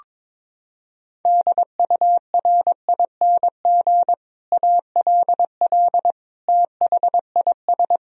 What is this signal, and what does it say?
Morse code, heard as a keyed tone, about 22 wpm, 700 Hz. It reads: DURING ALL THIS
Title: Sentences from Top 1000 Words - Review Part 2 - 22wpm